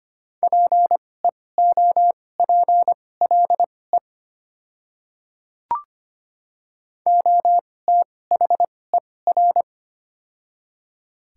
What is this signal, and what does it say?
Morse code, heard as a keyed tone, about 25 wpm, spaced as for 18 wpm, 690 Hz. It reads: PEOPLE OTHER
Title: 2x Character Spacing - Top 200 Words - 25wpm